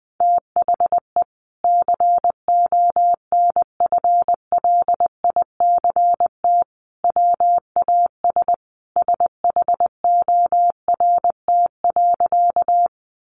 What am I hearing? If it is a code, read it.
Morse code, heard as a keyed tone, about 20 wpm, 700 Hz. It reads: THE CONFLICT WAS SHORT.
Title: Sentences from Top 1300 Words 20wpm